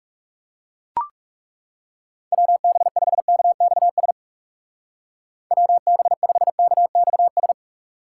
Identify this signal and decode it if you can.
Morse code, heard as a keyed tone, about 36 wpm, 700 Hz. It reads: WB5KXS WB5KXS
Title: Speed Racing - USA Call Signs 30wpm